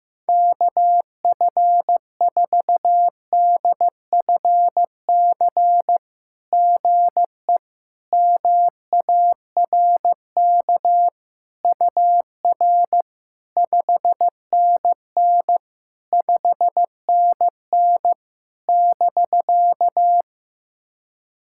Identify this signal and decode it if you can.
Morse code, heard as a keyed tone, about 16 wpm, 700 Hz. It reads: KF4DFC GE MARK UR 5NN 5NN <BK>
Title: SOTA QSO Practice 15wpm